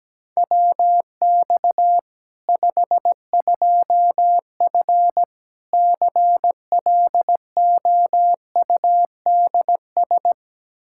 Morse code, heard as a keyed tone, about 17 wpm, 700 Hz. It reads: WX 52F CLOUDS
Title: K1USN SST QSO Practice 17wpm